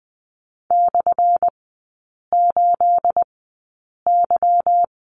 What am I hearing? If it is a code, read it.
Morse code, heard as a keyed tone, about 19 wpm, spaced as for 8 wpm, 700 Hz. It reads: /8Y
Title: 212 Lesson 34e - Number 8 - Three Characters 20wpm